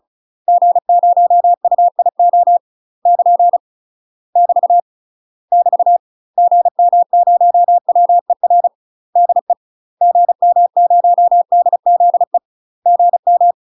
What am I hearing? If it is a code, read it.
Morse code, heard as a keyed tone, about 34 wpm, 700 Hz. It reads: G0UIO <KN> <BT> <BT> GM0WER DE GM0DZE GM
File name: UK MASTER QSOs TRAINING / 35WPM / UK MASTER QSOs 0004